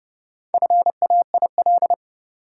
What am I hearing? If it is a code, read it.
Morse code, heard as a keyed tone, about 30 wpm, 700 Hz. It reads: FAIL
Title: Four Letter Words - 30wpm